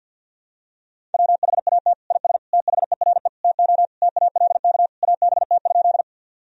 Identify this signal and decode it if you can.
Morse code, heard as a keyed tone, about 52 wpm, 700 Hz. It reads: WHAT IS THERE TO TALK ABT?
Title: Sentences from Top 400 Words and Abbreviations 50wpm